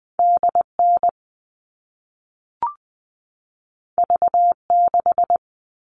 Morse code, heard as a keyed tone, about 20 wpm, 700 Hz. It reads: DN V6
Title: All Cut Numbers - 2-Digit 20wpm